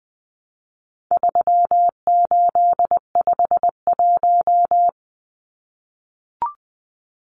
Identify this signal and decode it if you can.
Morse code, heard as a keyed tone, about 19 wpm, 700 Hz. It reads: 3851
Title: Numbers - 4-Digit 20wpm